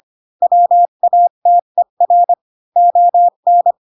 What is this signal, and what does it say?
Morse code, heard as a keyed tone, about 25 wpm, 700 Hz. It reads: WATER ON
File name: Pinocchio / Pinocchio ch7 20 25 ik1hge